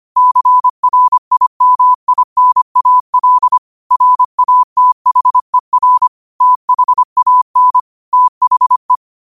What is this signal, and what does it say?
Morse code, heard as a keyed tone, about 25 wpm, 1000 Hz. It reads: CRIMINAL RATHER THAN THE
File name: qotdfiles / 25wpm